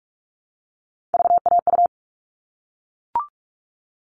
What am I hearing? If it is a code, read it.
Morse code, heard as a keyed tone, about 44 wpm, 700 Hz. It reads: VAU
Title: All Cut Numbers - 3-Digit 45wpm